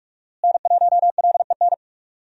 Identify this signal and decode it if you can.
Morse code, heard as a keyed tone, about 46 wpm, 700 Hz. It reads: N1LEN
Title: USA Call Signs - 5 Characters - Repeated 3x - 45wpm